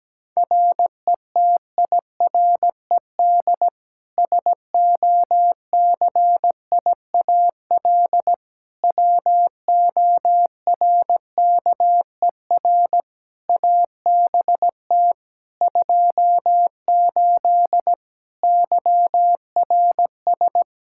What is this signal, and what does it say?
Morse code, heard as a keyed tone, about 17 wpm, 700 Hz. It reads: RETIRED SOCIAL WORKER ABT 28 YRS
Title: K1USN SST QSO Practice 17wpm